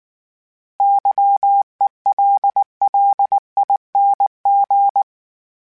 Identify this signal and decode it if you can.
Morse code, heard as a keyed tone, about 19 wpm, 800 Hz. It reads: YELLING